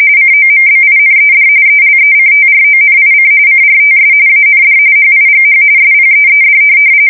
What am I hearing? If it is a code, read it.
RTTY, about 45.45 baud. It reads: COMBINED ELECTRONIC CRTS AND HIGH SPEED
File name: fsk170 / teletype-corp